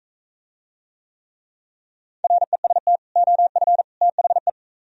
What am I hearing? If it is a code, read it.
Morse code, heard as a keyed tone, about 41 wpm, 700 Hz. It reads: REST OF THE